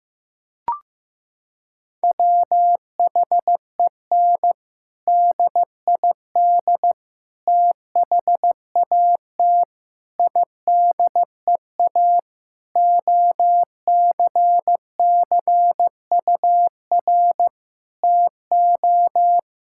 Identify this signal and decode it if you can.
Morse code, heard as a keyed tone, about 16 wpm, 700 Hz. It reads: WHEN DID THAT IDEA OCCUR TO
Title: Sentences from Top 1000 Words - Review Part 6 - 15wpm